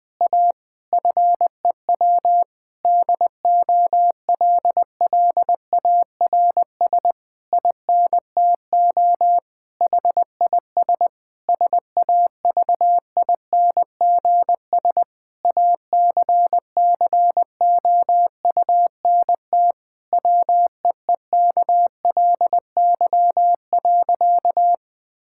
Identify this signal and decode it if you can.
Morse code, heard as a keyed tone, about 21 wpm, 700 Hz. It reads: A FEW DOLLARS INTO HIS SAVINGS ACCOUNT WEEKLY.
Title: Difficult Sentences with Hints - 20wpm